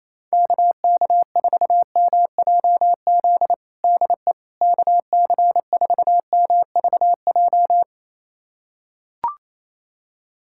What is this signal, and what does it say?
Morse code, heard as a keyed tone, about 28 wpm, 700 Hz. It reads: KK4MJZ DE KC4MVJ
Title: USA Callsign DE Callsign - 6 Characters - 28wpm